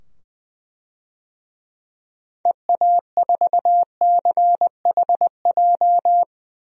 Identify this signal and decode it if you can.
Morse code, heard as a keyed tone, about 21 wpm, 700 Hz. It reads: EA4CHJ
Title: DXCC-100 Call Signs - 6 Characters - 20wpm